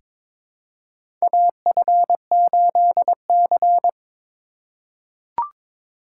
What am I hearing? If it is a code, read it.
Morse code, heard as a keyed tone, about 22 wpm, 700 Hz. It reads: AF8C
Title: USA Call Signs - 4 Characters - 2x Character Spacing - 22wpm